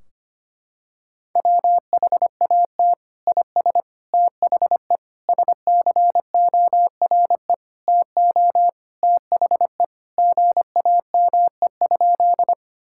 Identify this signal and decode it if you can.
Morse code, heard as a keyed tone, about 25 wpm, 700 Hz. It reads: WHAT IS THE SCORE TO THE GAME?
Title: Sentences from Top 1600 Words 25wpm